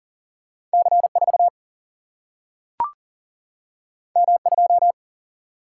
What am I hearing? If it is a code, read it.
Morse code, heard as a keyed tone, about 39 wpm, 700 Hz. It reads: C4 M2
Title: Single-Letter Single-Number 40wpm